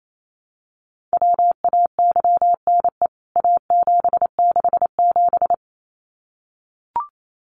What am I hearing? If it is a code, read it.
Morse code, heard as a keyed tone, about 28 wpm, 700 Hz. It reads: WAYNE A767
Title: CWT Exchange 28wpm